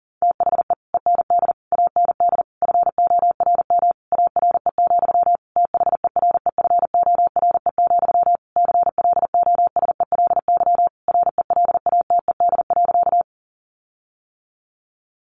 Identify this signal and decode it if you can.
Morse code, heard as a keyed tone, about 40 wpm, 700 Hz. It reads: THE END AND FORM ARE, THEREFORE, CLOSELY RELATED.
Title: Sentences from Top 1400 Words 40wpm